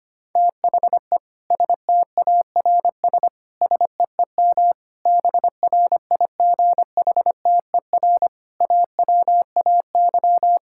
Morse code, heard as a keyed tone, about 25 wpm, 700 Hz. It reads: THE STARS SEEM BRIGHTER AWAY
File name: practice / 25wpm / Difficult Sentences with Hints - 25wpm